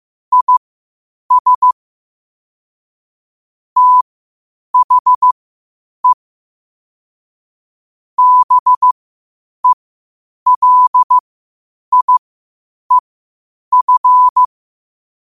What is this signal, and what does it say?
Morse code, heard as a keyed tone, about 15 wpm, spaced as for 8 wpm, 1000 Hz. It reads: IS THE BELIEF
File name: qotdfiles / 8wpm